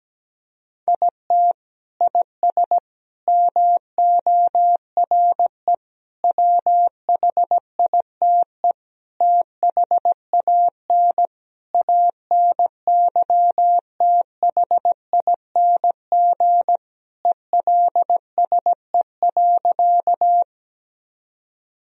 Morse code, heard as a keyed tone, about 17 wpm, 700 Hz. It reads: IT IS MORE WHITE THAN ANYTHING ELSE.
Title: Sentences from Top 1000 Words - Review Part 1 - 17wpm